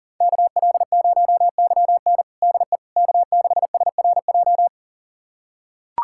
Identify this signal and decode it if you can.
Morse code, heard as a keyed tone, about 40 wpm, 700 Hz. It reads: KL0YN DE K6SRJ
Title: USA Callsign DE Callsign - 5 Characters - 40wpm